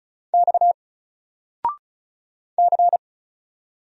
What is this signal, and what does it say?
Morse code, heard as a keyed tone, about 36 wpm, 700 Hz. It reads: X C
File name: Single-Letter-Number-35wpm